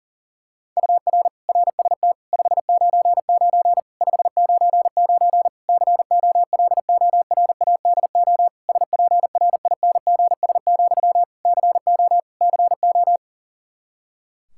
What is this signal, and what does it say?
Morse code, heard as a keyed tone, about 40 wpm, 700 Hz. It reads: UR RST 599 599 COLORADO SPRINGS, CO CO
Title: K1USN SST QSO Practice 40wpm